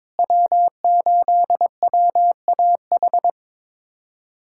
Morse code, heard as a keyed tone, about 23 wpm, 700 Hz. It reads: W8WAH
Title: USA Callsign DE Callsign - 5 Characters - 22wpm